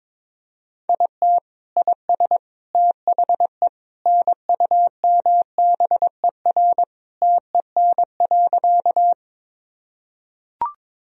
Morse code, heard as a keyed tone, about 22 wpm, 700 Hz. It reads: IT IS THE NUMBER TEN.
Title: Sentences from Top 1000 Words - Review Part 2 - 22wpm